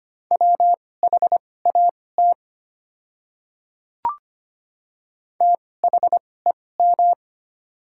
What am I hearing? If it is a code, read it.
Morse code, heard as a keyed tone, about 25 wpm, spaced as for 18 wpm, 700 Hz. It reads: WHAT THEM
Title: 2x Character Spacing - Top 100 Words - 25wpm